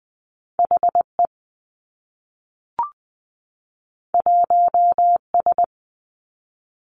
Morse code, heard as a keyed tone, about 20 wpm, 700 Hz. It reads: H E 1 S
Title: Two Challenging Letter-Number Combinations 20wpm